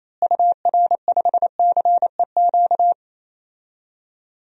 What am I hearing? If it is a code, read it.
Morse code, heard as a keyed tone, about 27 wpm, 700 Hz. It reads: UR5CEQ
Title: DXCC-100 Call Signs - 6 Characters - Repeated 2x - 28wpm